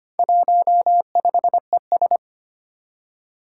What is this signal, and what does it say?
Morse code, heard as a keyed tone, about 25 wpm, 700 Hz. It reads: 15ES
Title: Four Challenging Letter-Number Combinations 25wpm